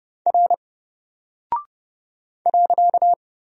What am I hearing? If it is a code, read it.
Morse code, heard as a keyed tone, about 30 wpm, 700 Hz. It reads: R .
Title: Single Character (Letter:Number:Punctuation:Prosign) Alt 30wpm